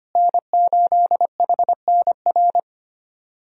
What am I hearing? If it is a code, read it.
Morse code, heard as a keyed tone, about 25 wpm, 700 Hz. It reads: N8HNR
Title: USA Call Signs - 5 Characters - Repeated 3x - 25wpm